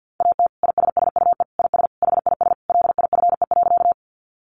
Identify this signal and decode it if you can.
Morse code, heard as a keyed tone, about 50 wpm, 700 Hz. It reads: AT ISSUE IS HIS LIFE.
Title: Sentences from Top 900 Words 50wpm